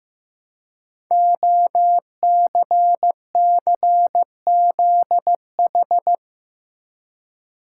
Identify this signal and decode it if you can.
Morse code, heard as a keyed tone, about 15 wpm, 700 Hz. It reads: OCCZH
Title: Code Groups - 5 Letters - 15wpm